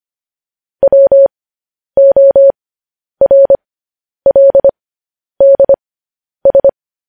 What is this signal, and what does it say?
Morse code, heard as a keyed tone, about 25 wpm, spaced as for 10 wpm, 550 Hz. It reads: WORLDS
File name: podcast / News headlines-1x-25x10